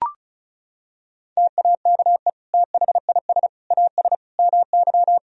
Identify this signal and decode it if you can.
Morse code, heard as a keyed tone, about 36 wpm, 700 Hz. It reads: TAKE THIS AS MY
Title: Sentences from Top 1000 Words - Review Part 2 - 35wpm